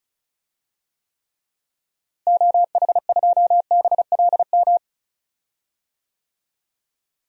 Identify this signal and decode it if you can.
Morse code, heard as a keyed tone, about 34 wpm, 700 Hz. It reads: OH2BLM
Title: DXCC-100 Call Signs - 6 Characters - Repeated 2x - 35wpm